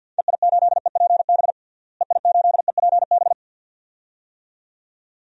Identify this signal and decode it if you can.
Morse code, heard as a keyed tone, about 51 wpm, 700 Hz. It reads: EI8EPB EI8EPB
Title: DXCC-100 Call Signs - 6 Characters - Repeated 2x - 50wpm